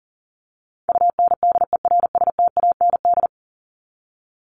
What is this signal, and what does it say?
Morse code, heard as a keyed tone, about 37 wpm, 700 Hz. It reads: UNDERSTAND
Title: ING Suffix - Top 100 Words with Base Words 40wpm